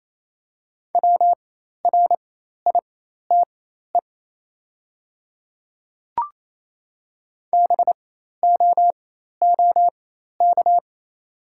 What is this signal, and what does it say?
Morse code, heard as a keyed tone, about 28 wpm, spaced as for 13 wpm, 700 Hz. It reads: WRITE BOOK
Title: 4x Character Spacing - Top 300 Words - 28wpm